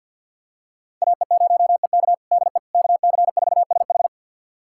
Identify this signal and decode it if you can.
Morse code, heard as a keyed tone, about 50 wpm, 700 Hz. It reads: AE0EK DE KC4SH
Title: USA Callsign DE Callsign - 5 Characters - 50wpm